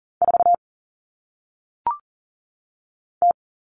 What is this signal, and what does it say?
Morse code, heard as a keyed tone, about 40 wpm, 700 Hz. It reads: 4 T
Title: All Cut Numbers - 1-Digit 40wpm